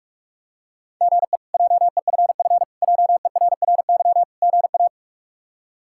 Morse code, heard as a keyed tone, about 41 wpm, 700 Hz. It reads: GE JEFF JERRY GA
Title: K1USN SST QSO Practice - 45wpm